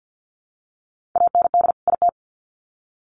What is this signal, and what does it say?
Morse code, heard as a keyed tone, about 48 wpm, 700 Hz. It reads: AND IT
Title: Sets of 2 Words 50wpm